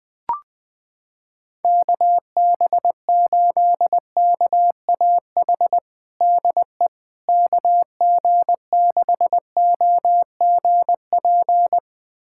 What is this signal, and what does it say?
Morse code, heard as a keyed tone, about 21 wpm, 700 Hz. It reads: KB8KAH DE KG6OGP
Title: USA Callsign DE Callsign - 6 Characters - 20wpm